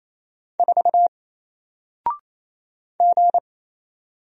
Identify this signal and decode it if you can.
Morse code, heard as a keyed tone, about 27 wpm, 700 Hz. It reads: 4 G
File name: Single-Letter-Number-28wpm